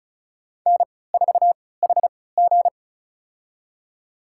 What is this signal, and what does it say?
Morse code, heard as a keyed tone, about 36 wpm, spaced as for 20 wpm, 700 Hz. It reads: N4HG
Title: USA Call Signs - 4 Characters - 3x Character Spacing - 35wpm